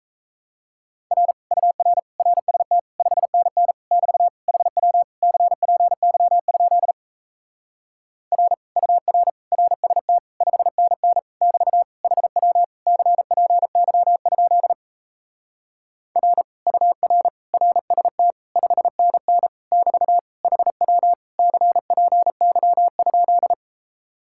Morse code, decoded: R UR RST 5NN <BT> HW CPY? R UR RST 5NN <BT> HW CPY? R UR RST 5NN <BT> HW CPY?

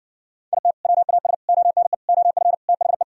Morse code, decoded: IT WAS ONE OF THE

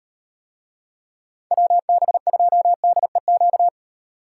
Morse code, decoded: WB2DEQ

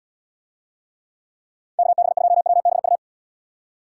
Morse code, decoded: KB2WBU

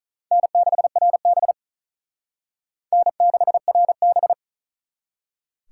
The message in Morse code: N6RB N6RB

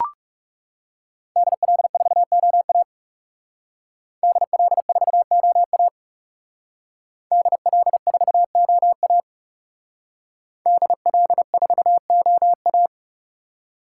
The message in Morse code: DL4OA DL4OA DL4OA DL4OA

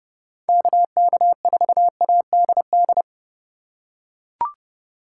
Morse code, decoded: KK4ADD

